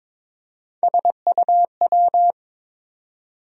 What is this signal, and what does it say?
Morse code, heard as a keyed tone, about 22 wpm, 700 Hz. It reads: SUW
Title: USA Call Sign Postfixes - 3 Letters 22wpm